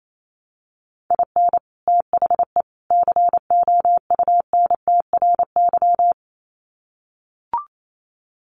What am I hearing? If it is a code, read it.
Morse code, heard as a keyed tone, about 28 wpm, 700 Hz. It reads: IN THE COUNTRY
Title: Sets of 3 Words 28wpm